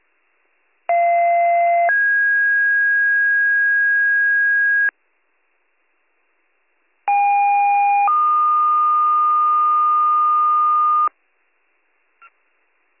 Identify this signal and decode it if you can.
Two-tone page, heard as a one-second tone, then a three-second tone, about 690 then 1650 Hz, about 810 then 1220 Hz. It is heard from 0.9 to 11.1 s.